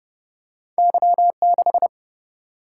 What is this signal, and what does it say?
Morse code, heard as a keyed tone, about 30 wpm, 700 Hz. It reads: Y 6
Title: Single-Letter Single-Number 30wpm